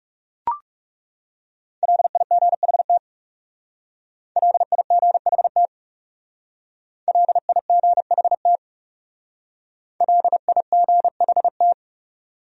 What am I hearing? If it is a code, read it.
Morse code, heard as a keyed tone, about 38 wpm, 700 Hz. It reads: LIGHT LIGHT LIGHT LIGHT